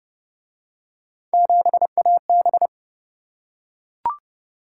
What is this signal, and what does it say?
Morse code, heard as a keyed tone, about 30 wpm, 700 Hz. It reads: 7AB